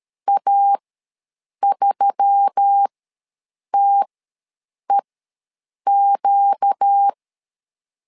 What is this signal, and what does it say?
Morse code, heard as a keyed tone, about 13 wpm, spaced as for 7 wpm, 800 Hz. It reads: A3TEQ
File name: LEZIONI / 4 Q1S3 ws 12wpm 2038